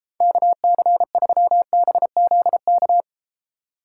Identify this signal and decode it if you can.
Morse code, heard as a keyed tone, about 33 wpm, 700 Hz. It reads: KC3BZK